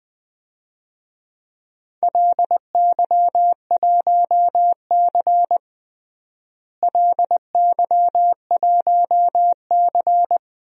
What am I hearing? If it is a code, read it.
Morse code, heard as a keyed tone, about 20 wpm, 700 Hz. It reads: LY1C LY1C
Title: DXCC-100 Call Signs - 4 Characters - Repeated 3x - 20wpm